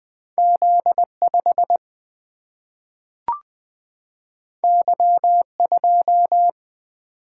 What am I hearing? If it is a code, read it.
Morse code, heard as a keyed tone, about 20 wpm, 700 Hz. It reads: Z5 Y2
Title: Single-Letter Single-Number 20wpm